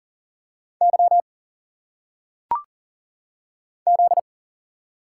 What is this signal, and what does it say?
Morse code, heard as a keyed tone, about 42 wpm, 700 Hz. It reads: Y Z
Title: Single Letters - Phonetic Pronunciation 40wpm